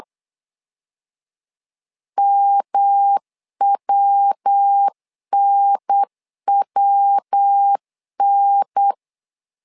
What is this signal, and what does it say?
Morse code, heard as a keyed tone, about 8 wpm, 800 Hz. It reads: MWNWN
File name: LEZIONI / 3 MWRO / 3 MW 8wpm 2501